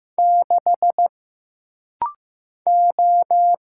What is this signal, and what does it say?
Morse code, heard as a keyed tone, about 15 wpm, 700 Hz. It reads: 6 O
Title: Single Letter-Number Alt - Rapid-Fire 15wpm